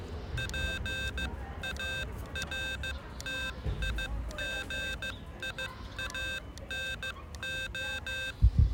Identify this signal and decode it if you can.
Morse code, heard as a keyed tone, about 15 wpm, spaced as for 13 wpm, 1550 Hz. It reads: PARTIGIANO